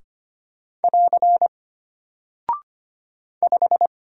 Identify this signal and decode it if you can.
Morse code, heard as a keyed tone, about 25 wpm, 700 Hz. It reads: <AR> 5
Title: Single Character (Letter:Number:Punctuation:Prosign) Alt 25wpm